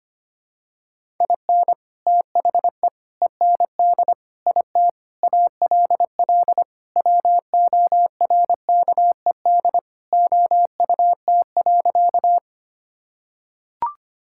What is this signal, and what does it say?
Morse code, heard as a keyed tone, about 25 wpm, 700 Hz. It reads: IN THE END IT ALL WORKED OUT.
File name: practice / 25wpm / Sentences with 3-Word Phrases - 25wpm